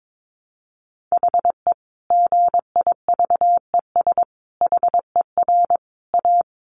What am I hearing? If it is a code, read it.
Morse code, heard as a keyed tone, about 22 wpm, 700 Hz. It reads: HE GIVES HER A
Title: Additional Easy Phrases 22wpm